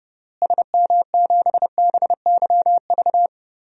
Morse code, heard as a keyed tone, about 32 wpm, 700 Hz. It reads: SM7BYV
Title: DXCC-100 Call Signs - 6 Characters - Repeated 3x - 30wpm